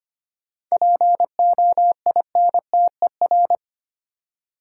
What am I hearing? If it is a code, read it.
Morse code, heard as a keyed tone, about 25 wpm, 690 Hz. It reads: POINTER
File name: practice / 25wpm / Easy Words with Hints - 25wpm